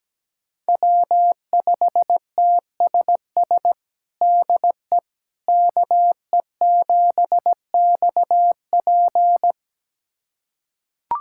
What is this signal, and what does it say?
Morse code, heard as a keyed tone, about 17 wpm, 700 Hz. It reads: W5TSS DE KE7XP
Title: USA Callsign DE Callsign 17wpm